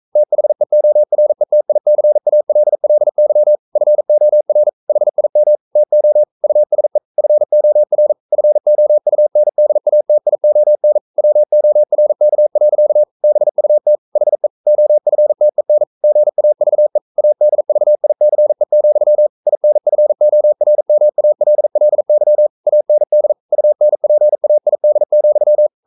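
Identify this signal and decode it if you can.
Morse code, heard as a keyed tone, about 42 wpm, 600 Hz. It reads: THEORETICALLY FOR HIM TO USE FOR FOUNDATION WORK. BUT HE OFTEN GAVE ADVICE, INFORMALLY AND UNPAID,
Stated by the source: Morse code practice at 41 wpm, from CW med Gnister.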